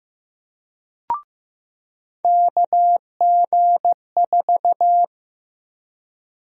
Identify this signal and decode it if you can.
Morse code, heard as a keyed tone, about 15 wpm, 700 Hz. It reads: KG4
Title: USA Call Sign Prefixes - 15wpm